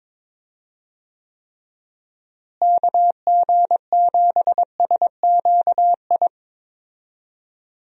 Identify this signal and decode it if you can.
Morse code, heard as a keyed tone, about 22 wpm, 700 Hz. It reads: KG7SQI
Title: USA Call Signs - 6 Characters - Repeated 3x - 22wpm